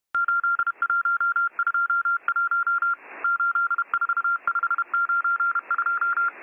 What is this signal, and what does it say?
Morse code, heard as a keyed tone, about 31 wpm, 1390 Hz. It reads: 8121 84592